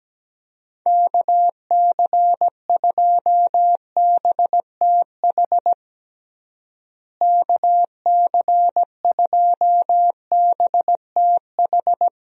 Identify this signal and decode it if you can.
Morse code, heard as a keyed tone, about 17 wpm, 700 Hz. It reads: KC2BTH KC2BTH
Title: USA Call Signs - 6 Characters - Repeated 2x - 17wpm